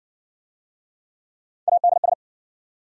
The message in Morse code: ADS

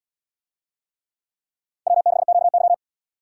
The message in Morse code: W6QY

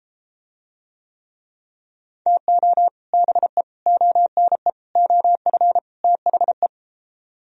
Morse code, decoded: TO BE ONE OF THE

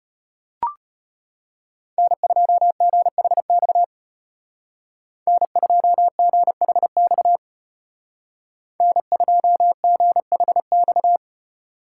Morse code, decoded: N2GHX N2GHX N2GHX

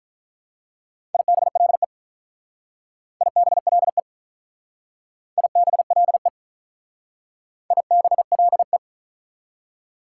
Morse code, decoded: IBLE IBLE IBLE IBLE

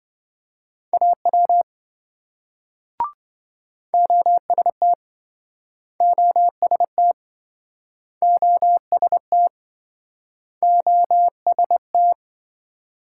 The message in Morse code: AW OST OST OST OST

